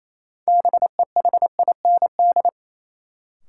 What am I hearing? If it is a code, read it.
Morse code, heard as a keyed tone, about 28 wpm, 700 Hz. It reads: BEHIND
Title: Sentences from Top 500 Words - New Word Drill - 28wpm